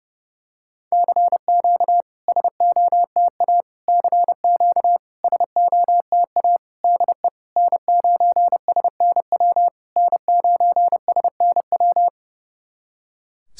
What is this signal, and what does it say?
Morse code, heard as a keyed tone, about 30 wpm, 700 Hz. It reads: CQ SOTA CQ SOTA DE N9SNW N9SNW
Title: SOTA QSO Practice 30wpm